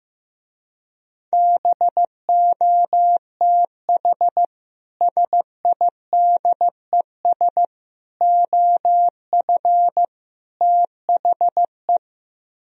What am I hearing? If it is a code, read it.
Morse code, heard as a keyed tone, about 16 wpm, 700 Hz. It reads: BOTH SIDES OF THE
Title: Speed-Racing - Sets of 4 Words 15wpm